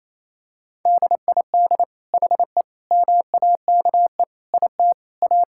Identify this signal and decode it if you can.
Morse code, heard as a keyed tone, about 28 wpm, 700 Hz. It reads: DID HE MAKE IT A
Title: Sentences from Top 1600 Words 28wpm